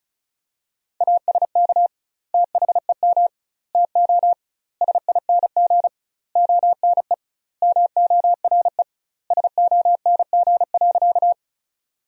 Morse code, heard as a keyed tone, about 34 wpm, 700 Hz. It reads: ASK THEM TO SING ONE MORE SONG.
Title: 2x Word-Spacing - Sentences from Top 400 Words - 35wpm